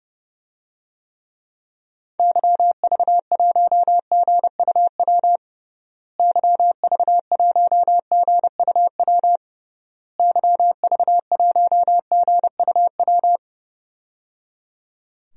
Morse code, heard as a keyed tone, about 30 wpm, 700 Hz. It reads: YV1GUW YV1GUW YV1GUW
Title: DXCC-100 Call Signs - 6 Characters - Repeated 3x - 30wpm